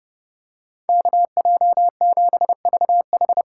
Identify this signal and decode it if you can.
Morse code, heard as a keyed tone, about 29 wpm, 700 Hz. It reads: KJ7VH